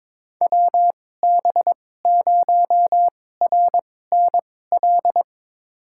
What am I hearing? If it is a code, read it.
Morse code, heard as a keyed tone, about 21 wpm, spaced as for 16 wpm, 700 Hz. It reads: WB0RNL
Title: USA Call Signs - 6 Characters - 2x Character Spacing - 22wpm